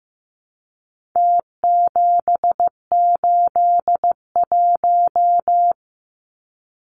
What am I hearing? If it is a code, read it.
Morse code, heard as a keyed tone, about 15 wpm, 700 Hz. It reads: T781
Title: Numbers - Contest Style - 4-Digit 15wpm